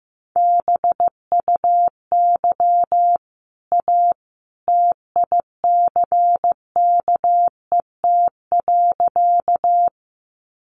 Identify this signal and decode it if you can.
Morse code, heard as a keyed tone, about 15 wpm, 700 Hz. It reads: BUY A TICKET.
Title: Sentences from Top 1400 Words 15wpm